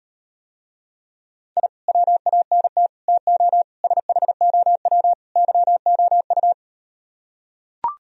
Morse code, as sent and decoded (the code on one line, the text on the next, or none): .. / .-- .- -. - / - --- / ... .... --- .-- / -.-- --- ..-
I WANT TO SHOW YOU